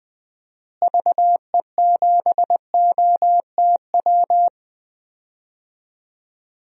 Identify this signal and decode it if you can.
Morse code, heard as a keyed tone, about 20 wpm, 700 Hz. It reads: VE7OTW
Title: DXCC-100 Call Signs - 6 Characters - Repeated 3x - 20wpm